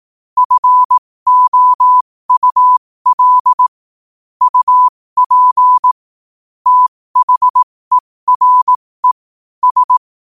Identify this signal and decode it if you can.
Morse code, heard as a keyed tone, about 18 wpm, spaced as for 15 wpm, 1000 Hz. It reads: FOUL UP THERE S
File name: 15wpm